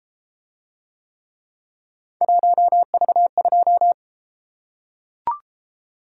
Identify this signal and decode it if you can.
Morse code, heard as a keyed tone, about 33 wpm, 700 Hz. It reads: 1V2